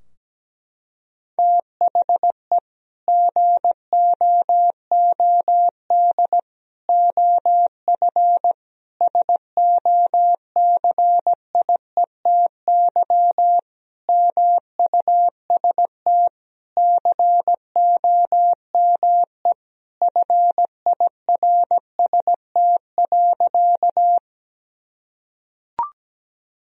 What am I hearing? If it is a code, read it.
Morse code, heard as a keyed tone, about 17 wpm, 700 Hz. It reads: THE GOOD OF SOCIETY MUST COME FIRST.
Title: Sentences from Top 1000 Words - Review Part 4 - 17wpm